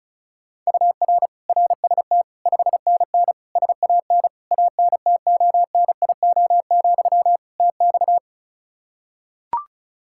Morse code, heard as a keyed tone, about 36 wpm, 700 Hz. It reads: UR RST 5NN SAN ANTONIO, TX
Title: Practice QSO - Exchange 1 - Beginning - 35wpm